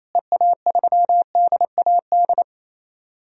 Morse code, heard as a keyed tone, about 28 wpm, 700 Hz. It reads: EA3DAD
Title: DXCC-100 Call Signs - 6 Characters - Repeated 3x - 28wpm